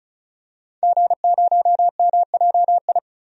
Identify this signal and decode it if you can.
Morse code, heard as a keyed tone, about 35 wpm, 700 Hz. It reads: G0MJI
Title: DXCC-100 Call Signs - 5 Characters - 35wpm